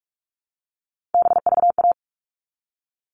Morse code, heard as a keyed tone, about 44 wpm, 700 Hz. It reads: BVA
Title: All Cut Numbers - 3-Digit 45wpm